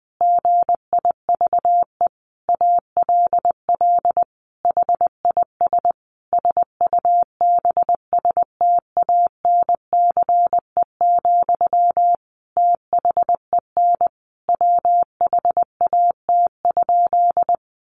Morse code, heard as a keyed tone, about 20 wpm, 700 Hz. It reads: GIVE ALL HIS SUBSTANCE, THEN WHAT?
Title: Sentences from Top 1500 Words 20wpm